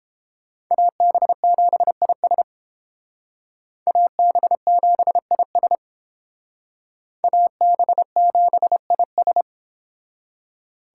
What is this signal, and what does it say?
Morse code, heard as a keyed tone, about 28 wpm, 700 Hz. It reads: AB7IS AB7IS AB7IS